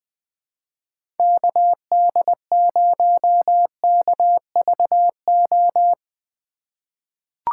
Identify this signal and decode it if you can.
Morse code, heard as a keyed tone, about 20 wpm, 700 Hz. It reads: KD0KVO
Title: USA Call Signs - 3x Character Spacing - 20wpm